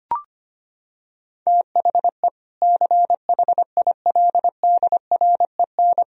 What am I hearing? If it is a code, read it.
Morse code, heard as a keyed tone, about 25 wpm, 700 Hz. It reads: THE CHILDREN
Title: Sentences from Top 1000 Words - Review Part 4 - 25wpm